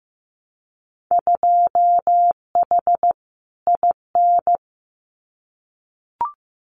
Morse code, heard as a keyed tone, about 16 wpm, 700 Hz. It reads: 2H IN
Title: Winter Field Day Exchanges Encore 15wpm